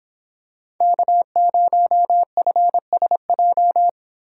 Morse code, heard as a keyed tone, about 24 wpm, 700 Hz. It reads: K0FSJ